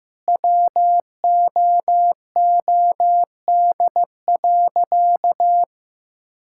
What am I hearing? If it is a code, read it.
Morse code, heard as a keyed tone, about 15 wpm, 700 Hz. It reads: WOOD.
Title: Sentences from Top 1000 Words - Review Part 6 - 15wpm